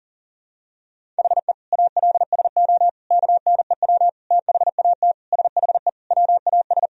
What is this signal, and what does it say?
Morse code, heard as a keyed tone, about 40 wpm, 700 Hz. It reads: HE ALSO KNEW THAT SHE WAS
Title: Sentences from Top 1000 Words - Review Part 1 - 40wpm